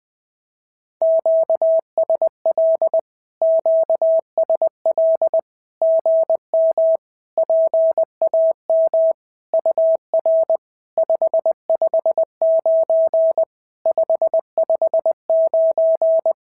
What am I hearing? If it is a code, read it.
Morse code, heard as a keyed tone, about 20 wpm, 650 Hz. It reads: QSL QSL GM PAM UR 559 559
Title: SOTA QSO Practice 20wpm